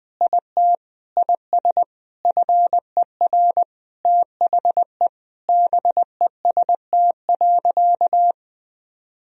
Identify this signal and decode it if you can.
Morse code, heard as a keyed tone, about 20 wpm, 700 Hz. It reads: IT IS FER THE BEST.
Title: Sentences from Top 500 Words and Abbreviations 20wpm